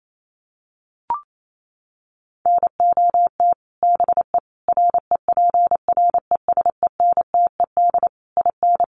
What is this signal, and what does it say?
Morse code, heard as a keyed tone, about 28 wpm, 700 Hz. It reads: NOT BE REPRESENTED IN
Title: Sets of 5 Words 28wpm